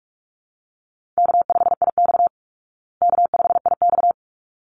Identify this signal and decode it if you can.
Morse code, heard as a keyed tone, about 44 wpm, 700 Hz. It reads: K5IX K5IX
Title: Call Signs Repeated 3x - 45wpm